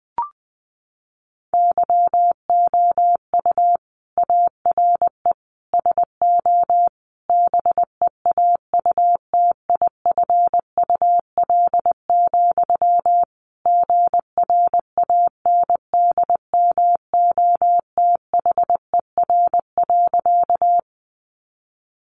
Morse code, heard as a keyed tone, about 20 wpm, 700 Hz. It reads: YOU ARE SO BEAUTIFUL, GRANDMOTHER.
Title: Sentences from Top 1200 Words 20wpm